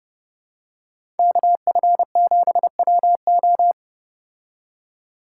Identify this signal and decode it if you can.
Morse code, heard as a keyed tone, about 30 wpm, 700 Hz. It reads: KF7WO